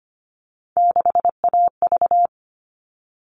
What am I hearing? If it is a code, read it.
Morse code, heard as a keyed tone, about 25 wpm, 700 Hz. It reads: 6AV
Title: All Cut Numbers - 3-Digit 25wpm